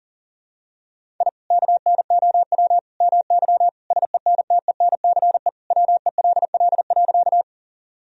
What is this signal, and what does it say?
Morse code, heard as a keyed tone, about 37 wpm, 700 Hz. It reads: I KNOW MY SENTENCE WELL.
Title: Sentences from Top 1000 Words - Review Part 1 - 40wpm